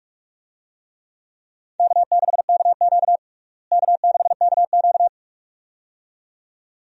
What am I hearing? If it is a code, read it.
Morse code, heard as a keyed tone, about 41 wpm, 700 Hz. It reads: K6KQ K6KQ
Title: USA Call Signs - 4 Characters - Repeated 2x - 45wpm